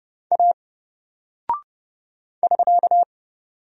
Morse code, heard as a keyed tone, about 30 wpm, 700 Hz. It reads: A <SK>
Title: Single Character (Letter:Number:Punctuation:Prosign) Alt 30wpm